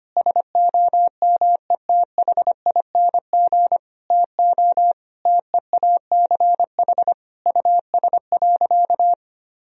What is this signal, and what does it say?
Morse code, heard as a keyed tone, about 25 wpm, 700 Hz. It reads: SOMETHING TO TEACH US.
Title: Sentences from Top 1000 Words - Review Part 2 - 25wpm